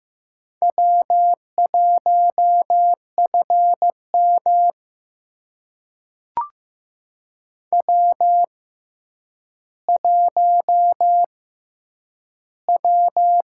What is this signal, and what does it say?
Morse code, heard as a keyed tone, about 15 wpm, spaced as for 9 wpm, 700 Hz. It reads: W1FM W1W
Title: USA Call Signs - 6x Character Spacing - 15wpm